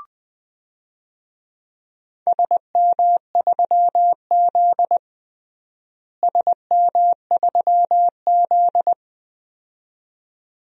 Morse code, heard as a keyed tone, about 20 wpm, 700 Hz. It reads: SM3Z SM3Z
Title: DXCC-100 Call Signs - 4 Characters - Repeated 2x - 20wpm